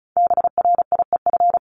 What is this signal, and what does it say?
Morse code, heard as a keyed tone, about 34 wpm, 700 Hz. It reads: BRIEF
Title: LY Suffix - Top 100 Words with Base Words 35wpm